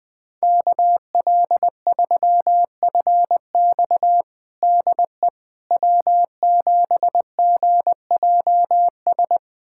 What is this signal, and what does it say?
Morse code, heard as a keyed tone, about 21 wpm, 700 Hz. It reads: KL3FX DE W7GJS
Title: USA Callsign DE Callsign - 5 Characters - 20wpm